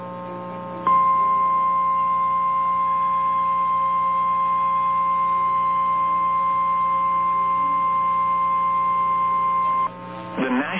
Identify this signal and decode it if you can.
Emergency Alert System (SAME) tones (the 1050 Hz attention tone), with speech. It is heard from 0.8 to 9.9 s.
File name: eas / KNCYKBIE / KNCYKBIE